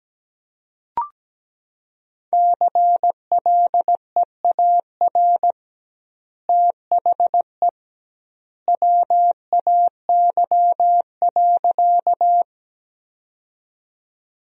Morse code, heard as a keyed tone, about 17 wpm, 700 Hz. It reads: CLEAR THE WAY.
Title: 2x Word-Spacing - Sentences from Top 500 Words - 17wpm